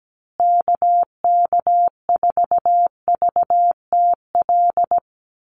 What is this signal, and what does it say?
Morse code, heard as a keyed tone, about 17 wpm, 700 Hz. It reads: KK4VTL
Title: USA Call Signs Repeated 2x 17wpm